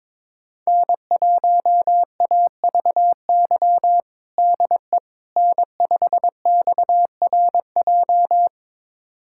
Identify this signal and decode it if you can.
Morse code, heard as a keyed tone, about 22 wpm, 700 Hz. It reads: N1AVY DE N5XRJ
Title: USA Callsign DE Callsign - 5 Characters - 22wpm